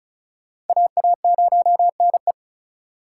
Morse code, heard as a keyed tone, about 43 wpm, 700 Hz. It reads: AA0NE
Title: Speed Racing - USA 2x2 Call Signs 35wpm